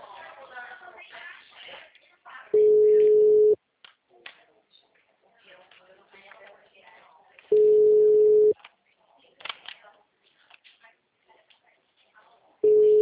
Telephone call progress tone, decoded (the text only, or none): ringback tone